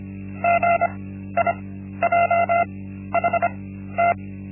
Morse code, heard as a keyed tone, about 25 wpm, spaced as for 13 wpm, 690 Hz. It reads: GIJHT
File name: Linksmakalnis-1